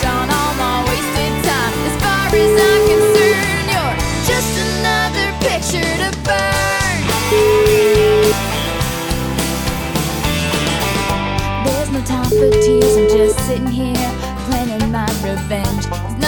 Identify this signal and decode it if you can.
Ringback tone (Europe, 425 Hz), with noise.